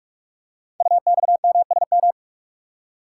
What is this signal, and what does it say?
Morse code, heard as a keyed tone, about 44 wpm, 700 Hz. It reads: UXMSM
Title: Code Groups - 5 Letters - 45wpm